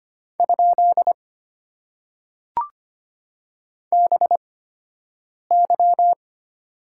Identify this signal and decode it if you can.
Morse code, heard as a keyed tone, about 25 wpm, spaced as for 7 wpm, 700 Hz. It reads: ? BY
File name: Top 90 QSO Elements - 8x Character Spacing - 25wpm